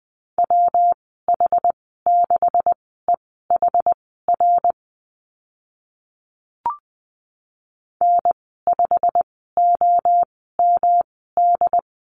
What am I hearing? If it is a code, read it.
Morse code, heard as a keyed tone, about 20 wpm, spaced as for 14 wpm, 700 Hz. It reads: WH6EHR N5OMD
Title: 128 Lesson 21g - Number 6 - Call Signs 20wpm